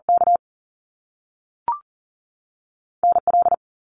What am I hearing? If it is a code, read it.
Morse code, heard as a keyed tone, about 40 wpm, 700 Hz. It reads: K NL
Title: Canadian Provinces and Territories 40wpm